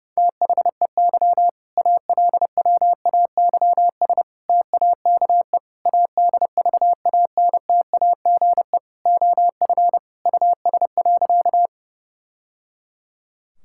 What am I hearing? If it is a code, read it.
Morse code, heard as a keyed tone, about 32 wpm, 700 Hz. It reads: THEY ALWAYS TAKE ADVANTAGE OF US.
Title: Sentences from Top 1000 Words - Review Part 6 - 30wpm